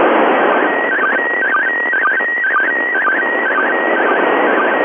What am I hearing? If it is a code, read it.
SSTV (slow-scan television), PD 120.